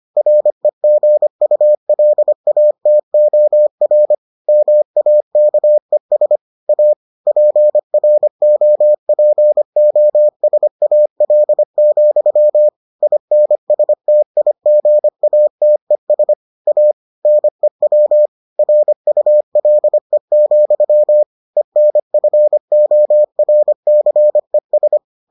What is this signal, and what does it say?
Morse code, heard as a keyed tone, about 25 wpm, 600 Hz. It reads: REGULATOR MAKES A PROPOSAL, INSTIGATES A NEW RULE, ENFORCES